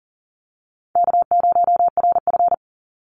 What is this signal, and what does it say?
Morse code, heard as a keyed tone, about 40 wpm, 700 Hz. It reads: K0RF
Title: Call Signs - 40wpm